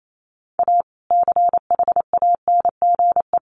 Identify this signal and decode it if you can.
Morse code, heard as a keyed tone, about 28 wpm, 700 Hz. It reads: A CHANGE
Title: Sentences from Top 1400 Words 28wpm